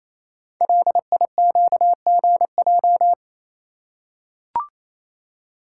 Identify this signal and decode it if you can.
Morse code, heard as a keyed tone, about 29 wpm, 700 Hz. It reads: LIQGJ
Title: Code Groups - 5 Letters - 28wpm